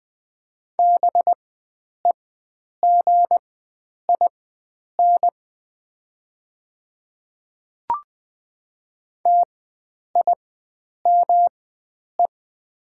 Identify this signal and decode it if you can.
Morse code, heard as a keyed tone, about 21 wpm, spaced as for 9 wpm, 700 Hz. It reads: BEGIN TIME
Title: 4x Character Spacing - Top 500 Words - 20wpm